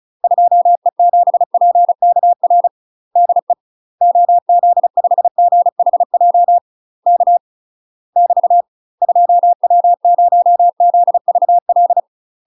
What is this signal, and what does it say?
Morse code, heard as a keyed tone, about 34 wpm, 700 Hz. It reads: 2E7PKR DE OZ5GHJ K <BT> 2W0ZVL
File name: UK MASTER QSOs TRAINING / 35WPM / UK MASTER QSOs 0041